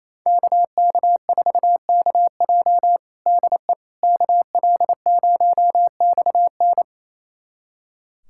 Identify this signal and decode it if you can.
Morse code, heard as a keyed tone, about 28 wpm, 700 Hz. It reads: KK4KJ DE KL0XN
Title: USA Callsign DE Callsign - 5 Characters - 28wpm